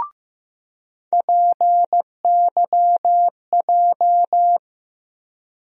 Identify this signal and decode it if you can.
Morse code, heard as a keyed tone, about 15 wpm, 700 Hz. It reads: PYJ
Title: USA Call Sign Postfixes - 3 Letters 15wpm